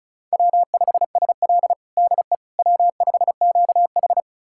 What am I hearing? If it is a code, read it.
Morse code, heard as a keyed tone, about 36 wpm, 700 Hz. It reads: W5SL DE W5QH
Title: USA Callsign DE Callsign - 4 Characters - 35wpm